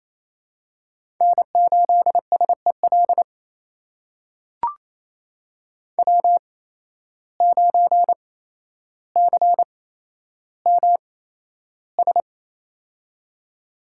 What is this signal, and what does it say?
Morse code, heard as a keyed tone, about 27 wpm, spaced as for 12 wpm, 700 Hz. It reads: N8SEL W9CMS
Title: USA Call Signs - 5 Characters - 8x Character Spacing - 28wpm